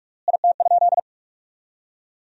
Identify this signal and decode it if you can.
Morse code, heard as a keyed tone, about 46 wpm, 700 Hz. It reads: IT?
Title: 3x Word-Spacing - Sentences from Top 400 Words - 45wpm